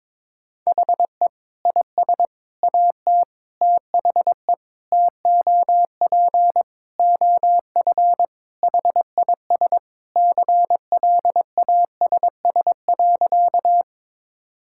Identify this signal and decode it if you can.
Morse code, heard as a keyed tone, about 22 wpm, 700 Hz. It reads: HE IS AT THE TOP OF HIS CLASS.
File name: practice / 22wpm / Sentences with 4-Word Phrases - 22wpm